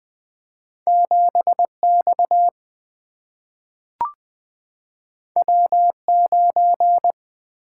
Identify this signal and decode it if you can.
Morse code, heard as a keyed tone, about 20 wpm, 700 Hz. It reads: 7 X W 9
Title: Single-Letter Single-Number 20wpm